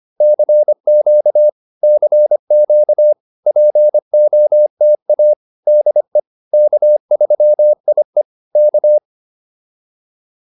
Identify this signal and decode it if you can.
Morse code, heard as a keyed tone, about 25 wpm, 600 Hz. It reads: CQ CQ POTA DE K3IE K
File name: POTA307a 25